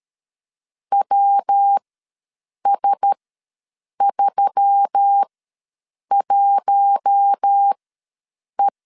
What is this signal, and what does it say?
Morse code, heard as a keyed tone, about 13 wpm, spaced as for 7 wpm, 800 Hz. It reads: WS31E
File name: LEZIONI / 5 LBVC / 5 L ws 12wpm 2958